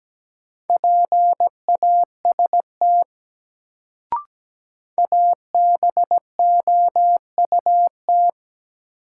Morse code, heard as a keyed tone, about 17 wpm, 700 Hz. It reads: PAST ABOUT
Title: Sets of 2 Words - New Word Drill - 17wpm